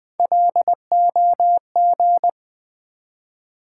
Morse code, heard as a keyed tone, about 20 wpm, 700 Hz. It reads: LOG
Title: Three Letter Words - 20wpm